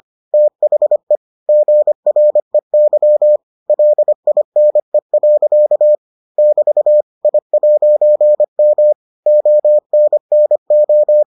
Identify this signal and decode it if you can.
Morse code, heard as a keyed tone, about 25 wpm, 600 Hz. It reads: THE GREY LINE. <BT> I'M ONNO